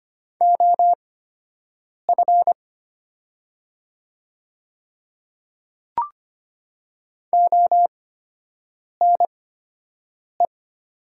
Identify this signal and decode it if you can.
Morse code, heard as a keyed tone, about 25 wpm, spaced as for 7 wpm, 700 Hz. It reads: OF ONE
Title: 8x Character Spacing - Top 200 Words - 25wpm